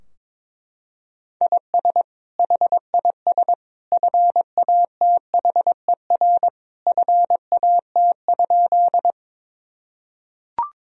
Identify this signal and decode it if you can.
Morse code, heard as a keyed tone, about 22 wpm, 700 Hz. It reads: IS HIS FATHER FAT?
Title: Sentences from Top 1000 Words - Review Part 4 - 22wpm